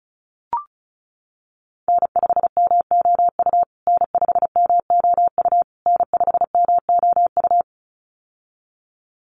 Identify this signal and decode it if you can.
Morse code, heard as a keyed tone, about 34 wpm, 700 Hz. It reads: N5MOU N5MOU N5MOU
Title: USA Call Signs Repeated 3x 35wpm